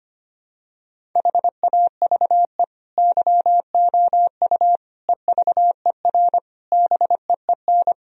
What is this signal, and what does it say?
Morse code, heard as a keyed tone, about 25 wpm, 700 Hz. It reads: HAVE YOU EVER BEEN
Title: Sentences from Top 1000 Words - Review Part 1 - 25wpm